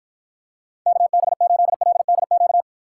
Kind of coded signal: Morse code

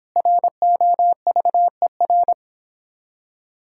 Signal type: Morse code

